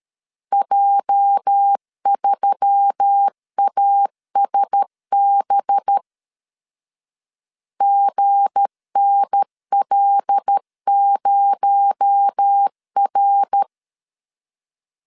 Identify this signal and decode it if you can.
Morse code, heard as a keyed tone, about 13 wpm, 800 Hz. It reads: J3ASB GNL0R
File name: CW / DET / 1 9 1500C S C 12wpm 3811